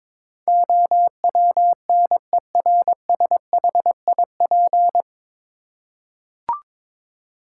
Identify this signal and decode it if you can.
Morse code, heard as a keyed tone, about 22 wpm, 700 Hz. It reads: OWNERSHIP
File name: 9 Letter Words - First 3 Letters Given - 22wpm